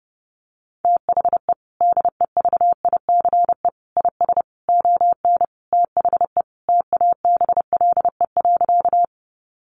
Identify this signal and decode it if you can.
Morse code, heard as a keyed tone, about 30 wpm, 700 Hz. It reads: THE DEVICE IS ON THE TABLE.
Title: Sentences from Top 1500 Words 30wpm